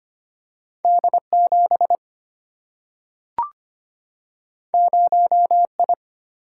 Morse code, heard as a keyed tone, about 25 wpm, 700 Hz. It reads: D7 0I
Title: Single-Letter Single-Number 25wpm